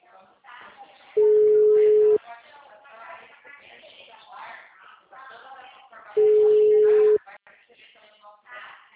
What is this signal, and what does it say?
Ringback tone (Europe, 425 Hz), with speech.